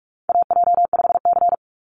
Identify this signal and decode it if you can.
Morse code, heard as a keyed tone, about 44 wpm, 690 Hz. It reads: AJ5C
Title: NA SOTA Chaser Callsigns 45wpm